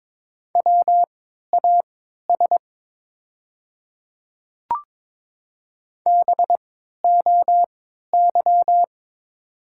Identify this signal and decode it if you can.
Morse code, heard as a keyed tone, about 22 wpm, spaced as for 13 wpm, 700 Hz. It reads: WAS BOY
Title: 3x Character Spacing - Top 300 Words - 22wpm